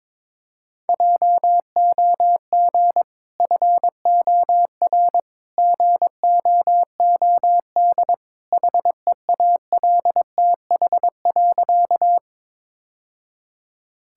Morse code, heard as a keyed tone, about 23 wpm, 700 Hz. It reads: JOG FOR GOOD HEALTH.
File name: practice / 22wpm / Easy Sentences with Hints - 22wpm